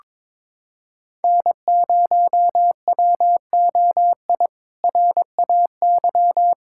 Morse code, heard as a keyed tone, about 22 wpm, 700 Hz. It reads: N0WOI RAY
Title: K1USN SST QSO Practice - 22wpm